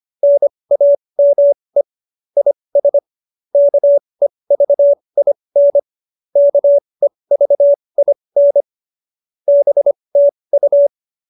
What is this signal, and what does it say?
Morse code, heard as a keyed tone, about 25 wpm, spaced as for 20 wpm, 570 Hz. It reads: NAME IS KEVIN KEVIN BTU